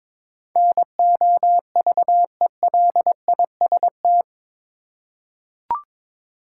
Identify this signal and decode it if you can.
Morse code, heard as a keyed tone, about 22 wpm, 700 Hz. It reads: NOVELIST
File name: practice / 22wpm / Eight Letter Words - 22wpm